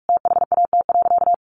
Morse code, heard as a keyed tone, about 44 wpm, 700 Hz. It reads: THAT.
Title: Sentences from Top 200 Words 45wpm